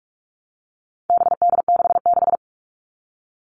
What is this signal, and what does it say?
Morse code, heard as a keyed tone, about 44 wpm, 700 Hz. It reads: BD66
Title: All Cut Numbers - 4-Digit 45wpm